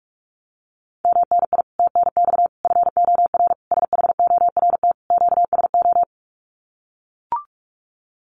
Morse code, heard as a keyed tone, about 44 wpm, 700 Hz. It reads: MNI TNX FOR SHORT QSO
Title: Common QSO Phrases 45wpm